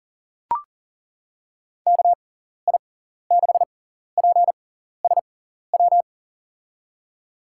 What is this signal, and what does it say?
Morse code, heard as a keyed tone, about 40 wpm, spaced as for 14 wpm, 700 Hz. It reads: KI6PSW
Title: USA Call Signs - 6 Characters - 6x Character Spacing - 40wpm